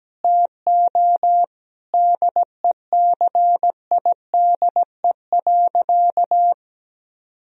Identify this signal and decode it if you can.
Morse code, heard as a keyed tone, about 17 wpm, 700 Hz. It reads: TO DECIDE.
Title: Sentences from Top 1000 Words - Review Part 3 - 17wpm